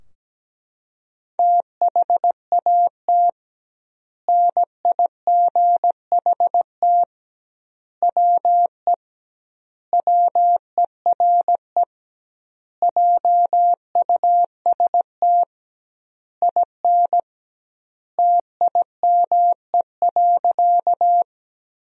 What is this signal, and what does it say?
Morse code, heard as a keyed tone, about 17 wpm, 700 Hz. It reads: THAT NIGHT WE WERE JUST IN TIME.
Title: 2x Word-Spacing - Sentences from Top 300 Words - 17wpm